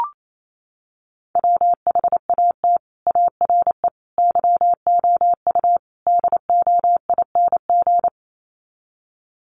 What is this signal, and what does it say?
Morse code, heard as a keyed tone, about 28 wpm, 700 Hz. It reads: WHAT ARE YOU DOING
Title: Sets of 4 Words 28wpm